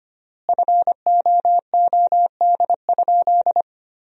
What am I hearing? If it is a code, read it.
Morse code, heard as a keyed tone, about 25 wpm, 700 Hz. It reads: FOOD?
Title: 4x Word-Spacing - Sentences from Top 300 Words - 25wpm